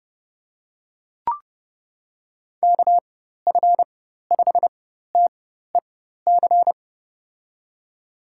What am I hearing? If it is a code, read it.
Morse code, heard as a keyed tone, about 30 wpm, spaced as for 14 wpm, 700 Hz. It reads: KF5TEC
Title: USA Call Signs - 6 Characters - 4x Character Spacing - 30wpm